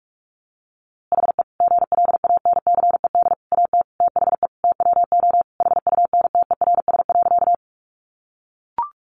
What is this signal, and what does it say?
Morse code, heard as a keyed tone, about 44 wpm, 700 Hz. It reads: HE GLANCED AT THE TWO HUNTERS.
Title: A Sound Of Thunder 45wpm